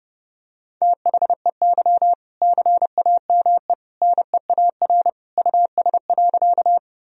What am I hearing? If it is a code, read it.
Morse code, heard as a keyed tone, about 32 wpm, 700 Hz. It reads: THEY CAME NEAR US.
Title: Sentences from Top 1000 Words - Review Part 1 - 30wpm